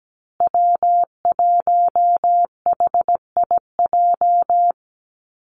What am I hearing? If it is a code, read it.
Morse code, heard as a keyed tone, about 17 wpm, 700 Hz. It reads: W1HIJ
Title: Call Signs Repeated 2x - 17wpm